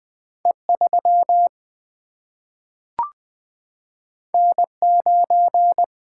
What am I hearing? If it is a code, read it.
Morse code, heard as a keyed tone, about 20 wpm, 700 Hz. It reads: E3 N9
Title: Single-Letter Single-Number 20wpm